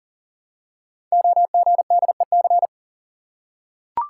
Morse code, heard as a keyed tone, about 42 wpm, 690 Hz. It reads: OGDEC
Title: Code Groups - 5 Letters - 40wpm